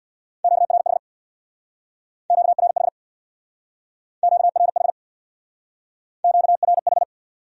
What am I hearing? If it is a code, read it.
Morse code, heard as a keyed tone, about 59 wpm, 700 Hz. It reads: QRH QRH QRH QRH
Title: Speed-Racing Q-Codes - 50wpm